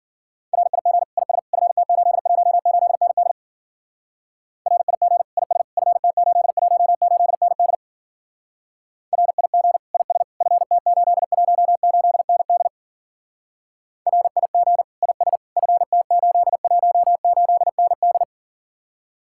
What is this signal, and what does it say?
Morse code, heard as a keyed tone, about 50 wpm, 700 Hz. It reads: RIG IS FT818ND RIG IS FT818ND RIG IS FT818ND RIG IS FT818ND
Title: Speed-Racing Common QSO Phrases Encore - 40wpm